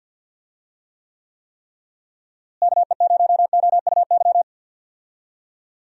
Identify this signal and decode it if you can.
Morse code, heard as a keyed tone, about 52 wpm, 700 Hz. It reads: KE0OUY
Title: USA Call Signs - 6 Characters - Repeated 3x - 50wpm